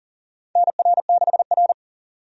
Morse code, heard as a keyed tone, about 40 wpm, 700 Hz. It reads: NR6R
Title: USA Call Signs - 4 Characters - 40wpm